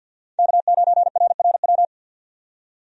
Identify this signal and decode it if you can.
Morse code, heard as a keyed tone, about 51 wpm, 700 Hz. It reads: K9RRW
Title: USA Call Signs - 5 Characters - 50wpm